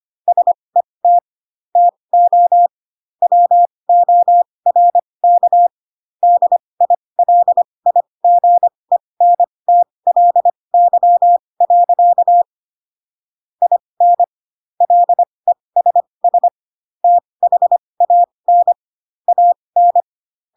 Morse code, heard as a keyed tone, about 25 wpm, spaced as for 20 wpm, 700 Hz. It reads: SET TO WORK DILIGENTLY. IN LESS THAN AN
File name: Pinocchio / Pinocchio ch8 15 25 ik1hge